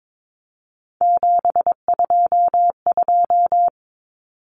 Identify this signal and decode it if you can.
Morse code, heard as a keyed tone, about 22 wpm, 700 Hz. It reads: 722
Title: Numbers - 3-Digit 22wpm